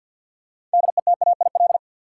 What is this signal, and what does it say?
Morse code, heard as a keyed tone, about 53 wpm, 700 Hz. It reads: DETAIL
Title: Sentences from Top 1500 Words - New Word Drill - 50wpm